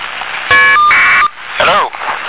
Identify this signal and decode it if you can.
Caller ID data burst, with speech.